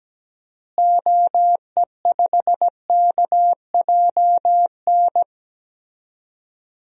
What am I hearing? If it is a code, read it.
Morse code, heard as a keyed tone, about 17 wpm, 700 Hz. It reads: OE5KJN
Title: DXCC-100 Call Signs - 6 Characters - Repeated 3x - 17wpm